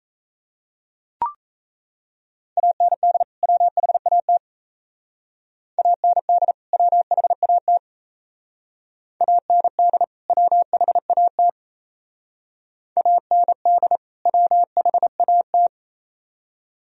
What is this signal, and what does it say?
Morse code, heard as a keyed tone, about 33 wpm, 700 Hz. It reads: AND WHAT AND WHAT AND WHAT AND WHAT
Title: Speed-Racing - Sets of 2 Words 28wpm